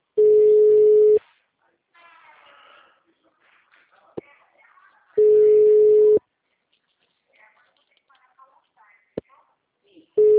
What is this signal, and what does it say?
Ringback tone (Europe, 425 Hz), with speech.